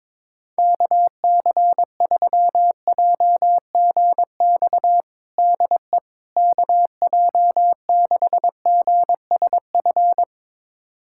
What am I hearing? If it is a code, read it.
Morse code, heard as a keyed tone, about 22 wpm, 700 Hz. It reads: KC3JGX DE KJ6GSF
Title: USA Callsign DE Callsign - 6 Characters - 22wpm